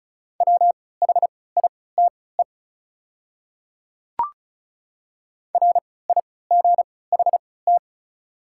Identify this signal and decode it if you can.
Morse code, heard as a keyed tone, about 34 wpm, spaced as for 20 wpm, 700 Hz. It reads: WHITE RIGHT
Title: 3x Character Spacing - Top 300 Words - 35wpm